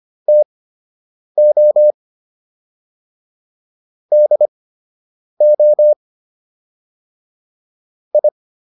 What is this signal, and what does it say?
Morse code, heard as a keyed tone, about 25 wpm, spaced as for 8 wpm, 600 Hz. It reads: TO DO I